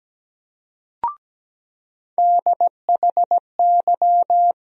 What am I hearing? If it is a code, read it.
Morse code, heard as a keyed tone, about 17 wpm, 700 Hz. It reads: DHY
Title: USA Call Sign Postfixes - 3 Letters 17wpm